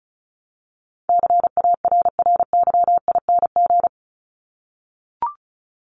Morse code, heard as a keyed tone, about 34 wpm, 700 Hz. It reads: CARRYING
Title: ING Suffix - Top 100 Words with Base Words 35wpm